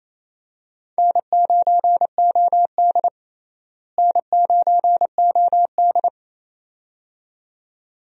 Morse code, heard as a keyed tone, about 27 wpm, 700 Hz. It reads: N9OD N9OD
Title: USA Call Signs - 4 Characters - Repeated 2x - 28wpm